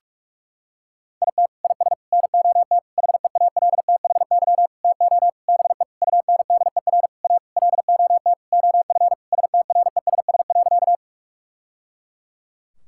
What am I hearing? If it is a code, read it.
Morse code, heard as a keyed tone, about 46 wpm, 700 Hz. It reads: IT IS NOT HEALTHY TO BE UNDER A LOT OF STRESS.
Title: Sentences from Top 1000 Words - Review Part 6 - 45wpm